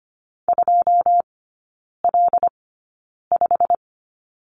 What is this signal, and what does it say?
Morse code, heard as a keyed tone, about 25 wpm, spaced as for 9 wpm, 700 Hz. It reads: 2L5
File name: Lesson 17e - Letter u - Three Characters 25wpm